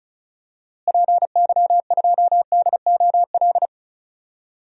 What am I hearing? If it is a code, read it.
Morse code, heard as a keyed tone, about 34 wpm, 700 Hz. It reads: PY2DOL
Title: DXCC-100 Call Signs - 6 Characters - Repeated 2x - 35wpm